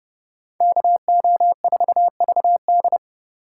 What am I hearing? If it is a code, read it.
Morse code, heard as a keyed tone, about 35 wpm, 700 Hz. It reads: KO4VD
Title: Speed Racing - USA 2x2 Call Signs 30wpm